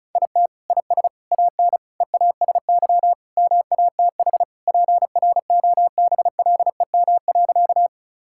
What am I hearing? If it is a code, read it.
Morse code, heard as a keyed tone, about 36 wpm, 700 Hz. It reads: IT IS AN EASY MATH PROBLEM.
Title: Sentences from Top 1000 Words - Review Part 6 - 35wpm